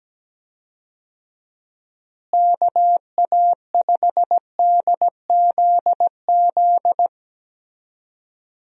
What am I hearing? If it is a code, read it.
Morse code, heard as a keyed tone, about 17 wpm, 700 Hz. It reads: KA5DZZ
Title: USA Call Signs - 6 Characters - Repeated 2x - 17wpm